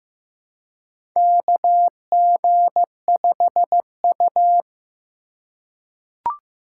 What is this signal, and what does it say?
Morse code, heard as a keyed tone, about 15 wpm, 700 Hz. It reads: KG5U
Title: USA Call Signs - 3x Character Spacing - 15wpm